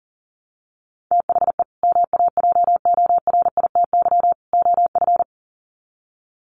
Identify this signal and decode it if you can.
Morse code, heard as a keyed tone, about 40 wpm, 700 Hz. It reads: THE MAJORITY OF
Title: Sets of 3 Words 40wpm